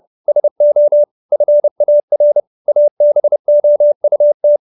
Morse code, heard as a keyed tone, about 30 wpm, 570 Hz. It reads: SO FAR ABOUT